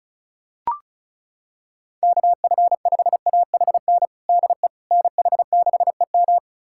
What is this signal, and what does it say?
Morse code, heard as a keyed tone, about 34 wpm, 700 Hz. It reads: KF5AHN DE NH6EM
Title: USA Callsign DE Callsign 35wpm